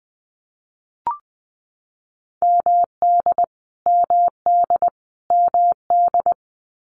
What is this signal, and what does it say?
Morse code, heard as a keyed tone, about 20 wpm, 700 Hz. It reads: MD MD MD
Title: US States Repeated 3x 20wpm